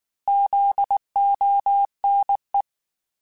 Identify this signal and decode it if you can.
Morse code, heard as a keyed tone, about 19 wpm, 800 Hz. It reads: ZONE
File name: Alphabetical / Alphabetical mp3s / Zs @18